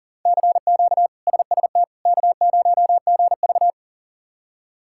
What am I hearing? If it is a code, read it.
Morse code, heard as a keyed tone, about 40 wpm, 700 Hz. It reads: CQ SST K0GV
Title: K1USN SST QSO Practice - 40wpm